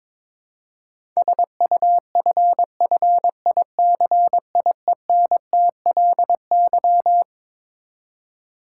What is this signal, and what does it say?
Morse code, heard as a keyed tone, about 21 wpm, 700 Hz. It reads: SUFFICIENTLY
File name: Twelve Letter Words - 22wpm